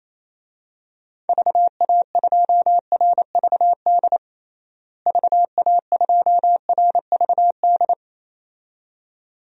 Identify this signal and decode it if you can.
Morse code, heard as a keyed tone, about 28 wpm, 700 Hz. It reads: VA2RVD VA2RVD
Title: DXCC-100 Call Signs - 6 Characters - Repeated 2x - 28wpm